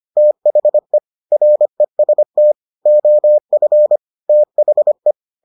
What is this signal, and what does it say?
Morse code, heard as a keyed tone, about 25 wpm, 600 Hz. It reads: THE REST OF THE